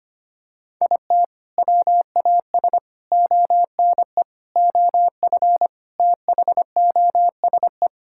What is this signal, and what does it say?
Morse code, heard as a keyed tone, about 25 wpm, 700 Hz. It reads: IT WAS ONE OF THOSE
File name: practice / 25wpm / Sentences with 4-Word Phrases - 25wpm